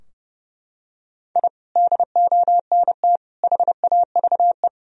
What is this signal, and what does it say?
Morse code, heard as a keyed tone, about 30 wpm, 700 Hz. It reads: I DONT HAVE
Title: Sentences from Top 1000 Words - Review Part 3 - 30wpm